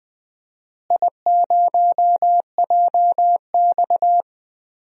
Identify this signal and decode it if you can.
Morse code, heard as a keyed tone, about 20 wpm, 700 Hz. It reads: I0JX
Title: DXCC-100 Call Signs - 4 Characters - Repeated 3x - 20wpm